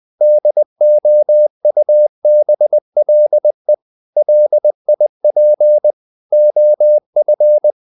Morse code, heard as a keyed tone, about 20 wpm, 600 Hz. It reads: DOUBLE LIP OF
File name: the old man and the sea part 2 20wpm